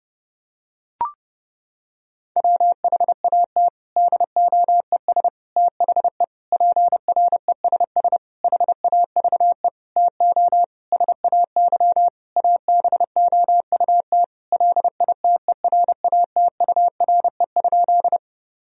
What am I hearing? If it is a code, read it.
Morse code, heard as a keyed tone, about 32 wpm, 700 Hz. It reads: WHAT DOES THE PRESS HAVE TO SAY ABOUT LITERATURE?
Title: Sentences from Top 1000 Words - Review Part 3 - 30wpm